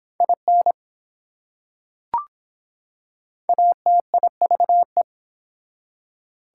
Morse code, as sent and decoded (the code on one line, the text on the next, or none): .. -. / .- - .. ...- .
IN ATIVE